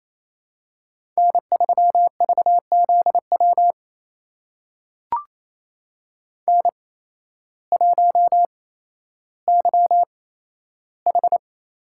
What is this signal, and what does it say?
Morse code, heard as a keyed tone, about 28 wpm, spaced as for 13 wpm, 700 Hz. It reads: N3VZW N1YH
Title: USA Call Signs - 5 Characters - 8x Character Spacing - 28wpm